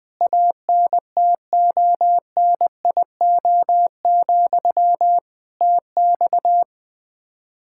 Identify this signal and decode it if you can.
Morse code, heard as a keyed tone, about 21 wpm, 700 Hz. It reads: ANTONIO, TX
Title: Practice QSO - Exchange 1 - Beginning - 20wpm